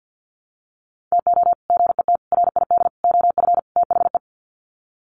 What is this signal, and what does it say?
Morse code, heard as a keyed tone, about 50 wpm, 700 Hz. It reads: TO GET RID OF THE
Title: Sets of 5 Words 50wpm